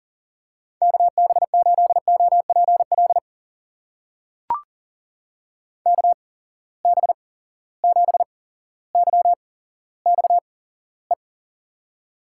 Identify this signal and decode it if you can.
Morse code, heard as a keyed tone, about 40 wpm, spaced as for 17 wpm, 700 Hz. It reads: KB8OPL KB7YXE
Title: USA Call Signs - 8x Character Spacing - 40wpm